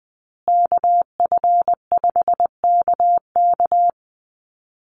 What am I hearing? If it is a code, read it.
Morse code, heard as a keyed tone, about 20 wpm, 700 Hz. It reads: KF5KK
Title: USA Call Signs Repeated 3x 20wpm